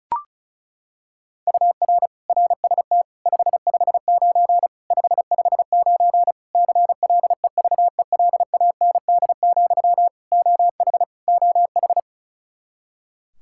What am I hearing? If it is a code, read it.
Morse code, heard as a keyed tone, about 36 wpm, 700 Hz. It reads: UR RST 559 559 CLEVELAND, OH OH
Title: K1USN SST QSO Practice 35wpm